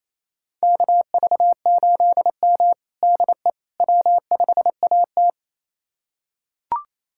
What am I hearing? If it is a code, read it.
Morse code, heard as a keyed tone, about 28 wpm, 700 Hz. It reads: KV8M DE W5AT
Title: USA Callsign DE Callsign - 4 Characters - 28wpm